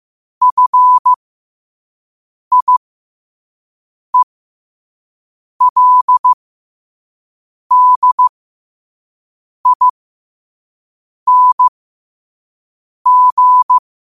Morse code, heard as a keyed tone, about 15 wpm, spaced as for 5 wpm, 1000 Hz. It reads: FIELDING
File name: qotdfiles / 5wpm